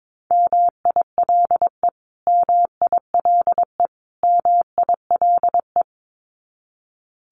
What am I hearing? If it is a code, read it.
Morse code, heard as a keyed tone, about 22 wpm, 700 Hz. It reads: MILE MILE MILE
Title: Top 300 Words Repeated 22wpm